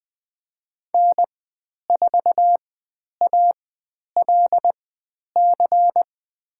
Morse code, heard as a keyed tone, about 20 wpm, spaced as for 10 wpm, 700 Hz. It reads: N4ALC
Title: 102 Lesson 17g - Letter u - Call Signs 20wpm